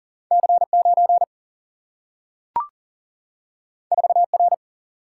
Morse code, heard as a keyed tone, about 39 wpm, 700 Hz. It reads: C9 4R
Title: Single-Letter Single-Number 40wpm